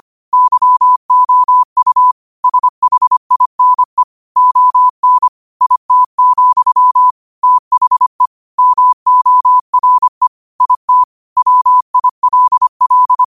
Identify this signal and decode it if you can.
Morse code, heard as a keyed tone, about 25 wpm, 1000 Hz. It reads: YOU SHINE ON IT, THE MORE IT WILL
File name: 25wpm